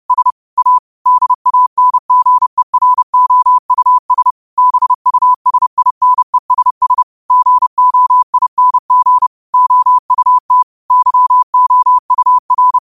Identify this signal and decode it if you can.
Morse code, heard as a keyed tone, about 30 wpm, 1000 Hz. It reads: S A DANGEROUS BUSINESS GOING OUT YOUR
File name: qotdfiles / 30wpm